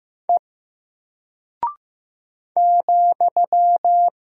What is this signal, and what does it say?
Morse code, heard as a keyed tone, about 15 wpm, 700 Hz. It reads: E ,
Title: Single Character (Letter:Number:Punctuation:Prosign) Alt 15wpm